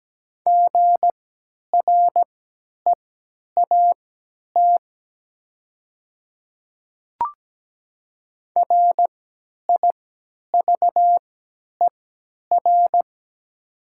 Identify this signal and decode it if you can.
Morse code, heard as a keyed tone, about 17 wpm, spaced as for 10 wpm, 700 Hz. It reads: GREAT RIVER
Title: 3x Character Spacing - Top 300 Words - 17wpm